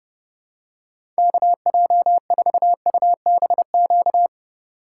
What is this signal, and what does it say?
Morse code, heard as a keyed tone, about 30 wpm, 700 Hz. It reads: KJ4UBQ
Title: USA Call Signs - 2x Character Spacing - 30wpm